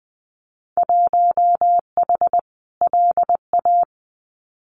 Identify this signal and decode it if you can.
Morse code, heard as a keyed tone, about 20 wpm, 700 Hz. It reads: 1H LA
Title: Winter Field Day Exchanges 20wpm